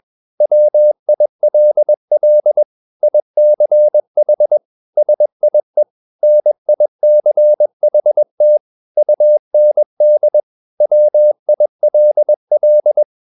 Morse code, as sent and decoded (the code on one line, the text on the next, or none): .-- .. .-.. .-.. / .. -.-. .... / ... .. / -. .. -.-. .... - / ..- -. -.. / .-- .. .-.. .-..
WILL ICH SIE NICHT UND WILL